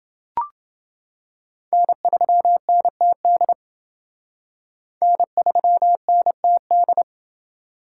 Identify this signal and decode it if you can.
Morse code, heard as a keyed tone, about 27 wpm, 700 Hz. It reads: N3NTD N3NTD